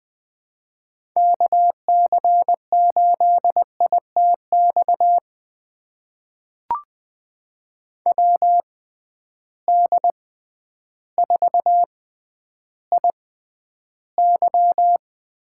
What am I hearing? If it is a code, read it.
Morse code, heard as a keyed tone, about 20 wpm, spaced as for 11 wpm, 700 Hz. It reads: KC8ITX WD4IY
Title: USA Call Signs - 6 Characters - 6x Character Spacing - 20wpm